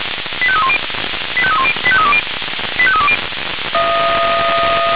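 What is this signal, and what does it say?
Five-tone selective call (ZVEI), heard as quick runs of five tones, tones of about 70 ms. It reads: 85310 85310 85320 85320
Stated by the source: ZVEI five-tone call 85878 received by SDR.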